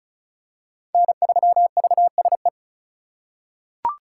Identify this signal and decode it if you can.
Morse code, heard as a keyed tone, about 36 wpm, 700 Hz. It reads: N3VSE
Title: USA Call Signs - 5 Characters - 2x Character Spacing - 35wpm